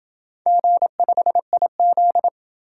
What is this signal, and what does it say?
Morse code, heard as a keyed tone, about 26 wpm, 700 Hz. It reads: G5IZ